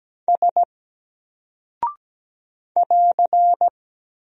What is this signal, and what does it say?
Morse code, heard as a keyed tone, about 17 wpm, 700 Hz. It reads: S <AR>
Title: Single Character (Letter:Number:Punctuation:Prosign) Alt 17wpm